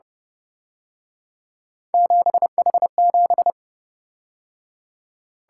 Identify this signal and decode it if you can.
Morse code, heard as a keyed tone, about 30 wpm, 700 Hz. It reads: 7H7
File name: Speed-Racing - Three Letter-Number Combinations - 25wpm